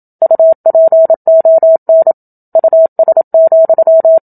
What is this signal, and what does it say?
Morse code, heard as a keyed tone, about 27 wpm, 650 Hz. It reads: UPON US,
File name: Book Two / 01-Chapter 1 - 27 WPM